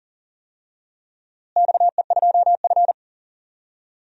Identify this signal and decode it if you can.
Morse code, heard as a keyed tone, about 40 wpm, 700 Hz. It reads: XE2F
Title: DXCC-100 Call Signs - 4 Characters - Repeated 3x - 40wpm